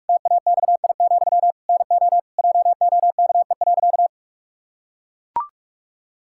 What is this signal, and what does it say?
Morse code, heard as a keyed tone, about 41 wpm, 700 Hz. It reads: TAXI, NO JOKE.
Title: Pangram Challenge v2 - 45wpm